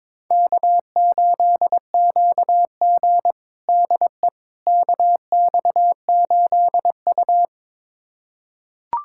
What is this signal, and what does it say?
Morse code, heard as a keyed tone, about 22 wpm, 700 Hz. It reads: K8QG DE KX8U
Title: USA Callsign DE Callsign - 4 Characters - 22wpm